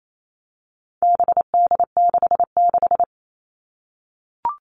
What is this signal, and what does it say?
Morse code, heard as a keyed tone, about 28 wpm, 700 Hz. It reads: BD66
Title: All Cut Numbers - 4-Digit 28wpm